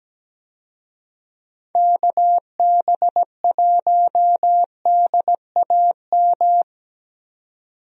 Morse code, heard as a keyed tone, about 17 wpm, 700 Hz. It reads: KB1DAM
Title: USA Call Signs - 6 Characters - 17wpm